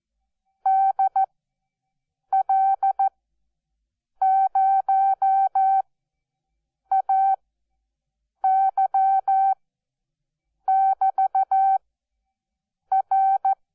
Morse code, heard as a keyed tone, about 14 wpm, spaced as for 6 wpm, 770 Hz. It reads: DL0AY<BT>R
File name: Onlinelehrgang / morsen / DJ4UF Morsen12